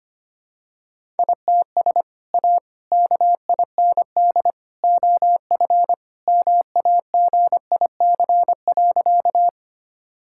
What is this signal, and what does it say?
Morse code, heard as a keyed tone, about 25 wpm, 700 Hz. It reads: ITS A KIND OF MAGIC.
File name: practice / 25wpm / Sentences with 3-Word Phrases - 25wpm